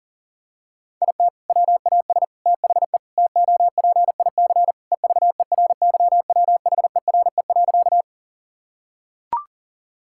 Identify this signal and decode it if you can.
Morse code, heard as a keyed tone, about 40 wpm, 700 Hz. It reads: IT WAS THE TOPIC EVERYWHERE.
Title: Sentences from Top 1600 Words 40wpm